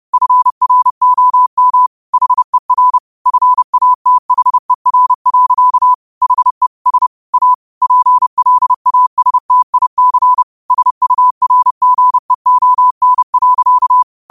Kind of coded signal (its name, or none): Morse code